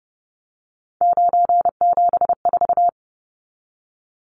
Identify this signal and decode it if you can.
Morse code, heard as a keyed tone, about 30 wpm, 700 Hz. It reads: 974
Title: Numbers - 3-Digit 30wpm